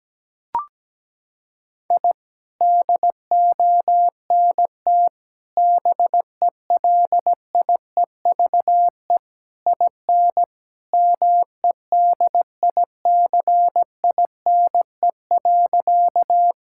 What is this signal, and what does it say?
Morse code, heard as a keyed tone, about 17 wpm, 700 Hz. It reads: I DONT BELIEVE IN MEDICINE.
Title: Sentences from Top 1000 Words - Review Part 6 - 17wpm